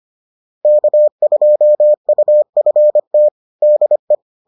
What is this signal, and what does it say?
Morse code, heard as a keyed tone, about 25 wpm, 600 Hz. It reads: K2UFT DE